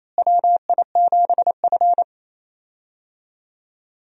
Morse code, heard as a keyed tone, about 28 wpm, 700 Hz. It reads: WI7F
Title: USA Call Signs - 4 Characters - Repeated 3x - 28wpm